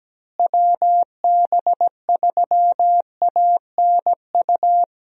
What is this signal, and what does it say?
Morse code, heard as a keyed tone, about 17 wpm, 700 Hz. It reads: WB3ANU
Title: USA Call Signs - 6 Characters - Repeated 2x - 17wpm